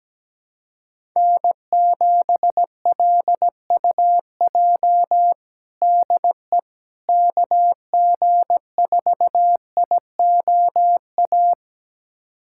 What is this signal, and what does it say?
Morse code, heard as a keyed tone, about 17 wpm, 700 Hz. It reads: N7LUJ DE KG4IOA
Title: USA Callsign DE Callsign 17wpm